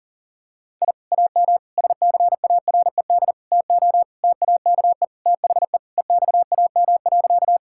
Morse code, heard as a keyed tone, about 40 wpm, 700 Hz. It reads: I AM SCARED TO TAKE THE EXAM.
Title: Sentences from Top 1000 Words - Review Part 5 - 40wpm